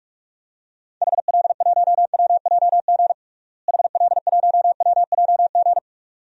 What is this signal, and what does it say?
Morse code, heard as a keyed tone, about 46 wpm, 700 Hz. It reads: HL1WJG HL1WJG
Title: DXCC-100 Call Signs - 6 Characters - Repeated 2x - 45wpm